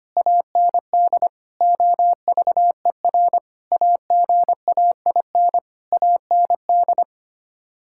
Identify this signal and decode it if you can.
Morse code, heard as a keyed tone, about 25 wpm, 700 Hz. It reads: AND OVER AGAIN AND
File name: Sets of 9 Words - 25wpm